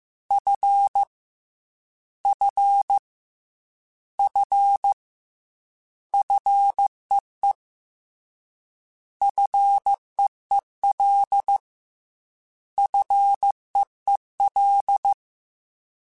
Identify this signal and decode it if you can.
Morse code, heard as a keyed tone, about 15 wpm, 800 Hz. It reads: F F F FEE FEEL FEEL